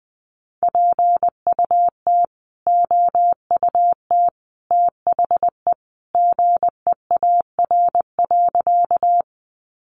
Morse code, heard as a keyed tone, about 20 wpm, 700 Hz. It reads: PUT OUT THE GEAR.
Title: Sentences from Top 1400 Words 20wpm